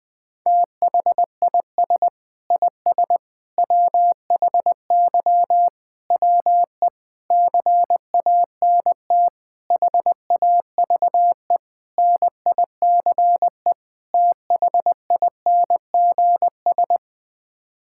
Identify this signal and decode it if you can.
Morse code, heard as a keyed tone, about 20 wpm, 700 Hz. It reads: THIS IS WHY WE CANT HAVE NICE THINGS
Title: Sets of 8 Words - 20wpm